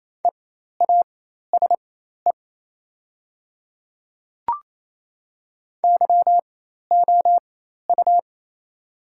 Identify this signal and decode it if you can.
Morse code, heard as a keyed tone, about 28 wpm, spaced as for 13 wpm, 700 Hz. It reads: EASE YOU
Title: 4x Character Spacing - Top 300 Words - 28wpm